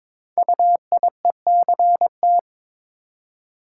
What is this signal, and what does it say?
Morse code, heard as a keyed tone, about 22 wpm, 700 Hz. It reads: UIECT
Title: Code Groups - 5 Letters - 22wpm